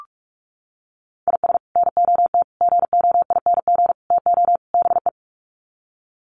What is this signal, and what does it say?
Morse code, heard as a keyed tone, about 44 wpm, 700 Hz. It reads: IS NOT GOING TO BE
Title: Sets of 5 Words 45wpm